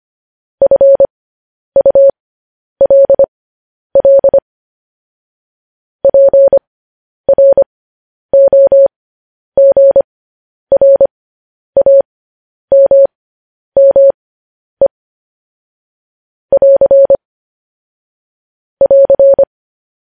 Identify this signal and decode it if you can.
Morse code, heard as a keyed tone, about 25 wpm, spaced as for 10 wpm, 550 Hz. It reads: FULL PROGRAMME <AR> <AR>
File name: podcast / News headlines-1x-25x10